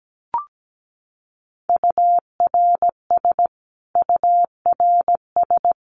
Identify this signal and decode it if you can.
Morse code, heard as a keyed tone, about 17 wpm, 700 Hz. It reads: URS URS
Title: Abbreviations Repeated 3x - 17wpm